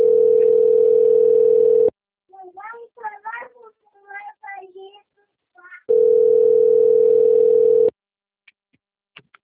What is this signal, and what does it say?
Ringback tone (North America), 440 + 480 Hz, with speech.
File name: rg-1600-unknown- - - .30471